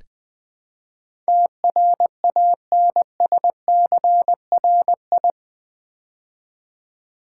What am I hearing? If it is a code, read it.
Morse code, heard as a keyed tone, about 20 wpm, 700 Hz. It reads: TRANSCRI
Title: Build A Word - 4 Letters At A Time - 10-Letter Words - 20wpm